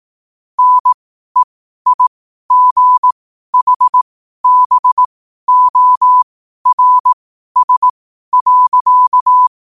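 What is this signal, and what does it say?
Morse code, heard as a keyed tone, about 18 wpm, spaced as for 12 wpm, 1000 Hz. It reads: NEIGHBORS.
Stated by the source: Morse code quote of the day at 12 wpm.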